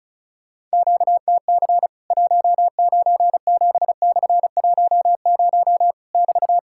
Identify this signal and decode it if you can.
Morse code, heard as a keyed tone, about 34 wpm, 700 Hz. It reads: QTC 197/10 <BT>
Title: Worked All Europe DX Contest - QTC Exchange - 35wpm